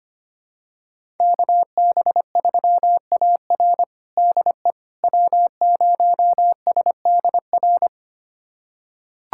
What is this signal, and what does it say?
Morse code, heard as a keyed tone, about 25 wpm, 700 Hz. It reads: KB3AR DE W0SDR
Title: USA Callsign DE Callsign - 5 Characters - 25wpm